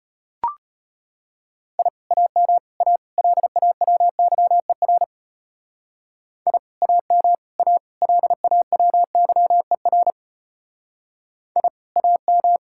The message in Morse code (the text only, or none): I AM A LAWYER I AM A LAWYER I AM